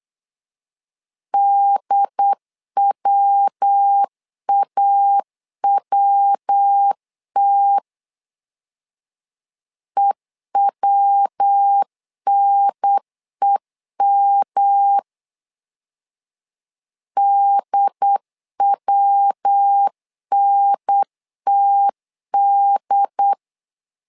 Morse code, heard as a keyed tone, about 8 wpm, 800 Hz. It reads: DWAWT EWNEM DWNTD
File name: LEZIONI / 3 MWRO / 3 MW 8wpm 2501